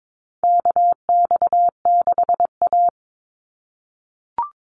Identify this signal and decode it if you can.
Morse code, heard as a keyed tone, about 22 wpm, 700 Hz. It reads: KX6A
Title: NA SOTA Chaser Callsigns 22wpm